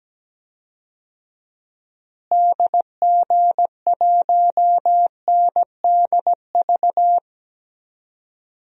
Morse code, heard as a keyed tone, about 17 wpm, 700 Hz. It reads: DG1NDV
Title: DXCC-100 Call Signs - 6 Characters - Repeated 3x - 17wpm